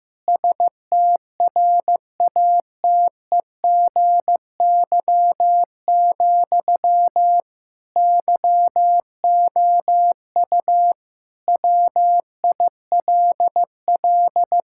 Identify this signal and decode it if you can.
Morse code, heard as a keyed tone, about 15 wpm, 700 Hz. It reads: STRATEGY, YOU WILL
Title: Sentences from Top 1000 Words - Review Part 5 - 15wpm